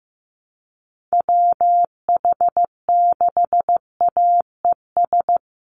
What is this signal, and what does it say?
Morse code, heard as a keyed tone, about 15 wpm, 700 Hz. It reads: WH6AES
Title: USA Call Signs Repeated 3x 15wpm